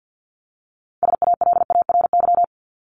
Morse code, heard as a keyed tone, about 50 wpm, 700 Hz. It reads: SALARY
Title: Top 1751-2000 Words 50wpm